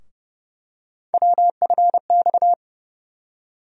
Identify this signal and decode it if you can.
Morse code, heard as a keyed tone, about 29 wpm, 700 Hz. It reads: WFX
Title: USA Call Sign Postfixes - 3 Letters 30wpm